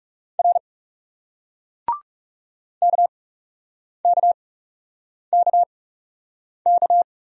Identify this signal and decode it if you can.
Morse code, heard as a keyed tone, about 40 wpm, spaced as for 8 wpm, 700 Hz. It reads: R KKKK